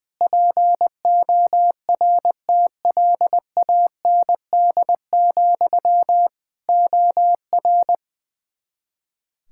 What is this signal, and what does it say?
Morse code, heard as a keyed tone, about 21 wpm, 700 Hz. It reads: PORTLAND, OR
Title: K1USN SST QSO Practice 20wpm